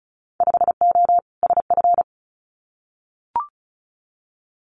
Morse code, heard as a keyed tone, about 36 wpm, 700 Hz. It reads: 5O SF
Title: Winter Field Day Exchanges Encore 35wpm